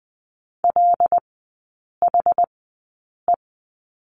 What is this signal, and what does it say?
Morse code, heard as a keyed tone, about 20 wpm, spaced as for 8 wpm, 700 Hz. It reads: LHE
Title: 074 Lesson 13e - Letter L - Three Characters 20wpm